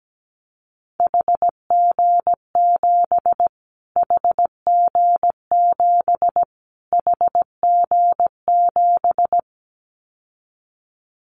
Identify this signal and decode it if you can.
Morse code, heard as a keyed tone, about 17 wpm, 700 Hz. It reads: HG7 HG7 HG7
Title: Callsign Prefixes for Top 25 DXCC - Repeated 3x - 17wpm